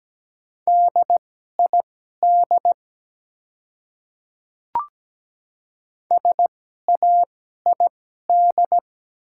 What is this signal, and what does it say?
Morse code, heard as a keyed tone, about 17 wpm, spaced as for 12 wpm, 700 Hz. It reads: DID SAID
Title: 2x Character Spacing - Top 100 Words - 17wpm